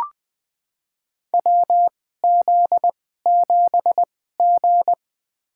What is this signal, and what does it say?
Morse code, heard as a keyed tone, about 20 wpm, spaced as for 15 wpm, 700 Hz. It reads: WZ7G
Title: USA Call Signs - 4 Characters - 2x Character Spacing - 20wpm